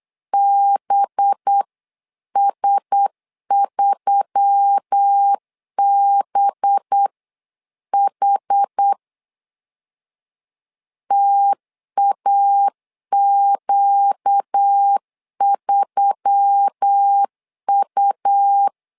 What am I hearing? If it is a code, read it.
Morse code, heard as a keyed tone, about 8 wpm, 800 Hz. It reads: BS3BH TAQ3U
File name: LEZIONI / 5 LBVC / 5 LB 8wpm 2816